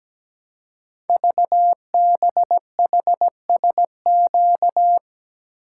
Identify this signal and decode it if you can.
Morse code, heard as a keyed tone, about 17 wpm, 700 Hz. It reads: VBHSQ
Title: Code Groups - 5 Letter:Numbers - 17wpm